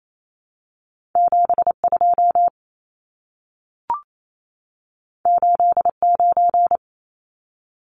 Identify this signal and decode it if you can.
Morse code, heard as a keyed tone, about 28 wpm, 700 Hz. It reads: 72 89
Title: Numbers - 2-Digit 28wpm